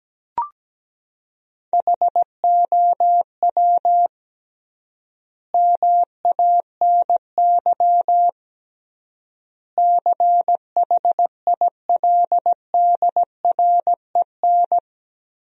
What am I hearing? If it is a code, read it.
Morse code, heard as a keyed tone, about 17 wpm, 700 Hz. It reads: HOW MANY CHILDREN
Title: 3x Word-Spacing - Sentences from Top 400 Words - 17wpm